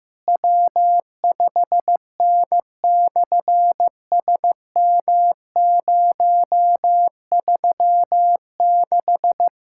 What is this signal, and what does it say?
Morse code, heard as a keyed tone, about 15 wpm, 700 Hz. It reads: W5N/SM036
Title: SOTA QSO Practice 15wpm